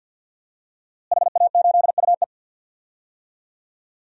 Morse code, heard as a keyed tone, about 52 wpm, 700 Hz. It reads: HA8UE
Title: DXCC-100 Call Signs - 5 Characters - Repeated 2x - 50wpm